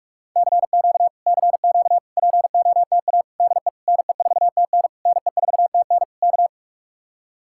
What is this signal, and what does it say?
Morse code, heard as a keyed tone, about 51 wpm, 700 Hz. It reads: CQ CQ POTA DE NE4TN NE4TN K